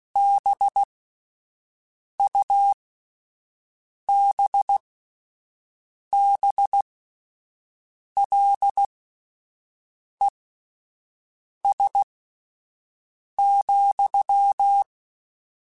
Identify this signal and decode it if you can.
Morse code, heard as a keyed tone, about 16 wpm, spaced as for 5 wpm, 800 Hz. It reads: BUBBLES,